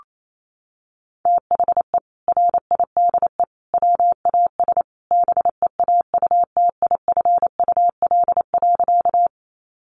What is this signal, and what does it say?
Morse code, heard as a keyed tone, about 28 wpm, 700 Hz. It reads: THE RIDE WAS BEAUTIFUL.
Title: Sentences from Top 1200 Words 28wpm